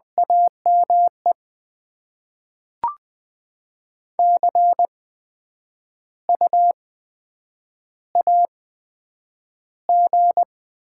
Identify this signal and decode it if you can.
Morse code, heard as a keyed tone, about 20 wpm, spaced as for 9 wpm, 700 Hz. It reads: AME CU AG
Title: Top 20 QSO Elements - 8x Character Spacing - 20wpm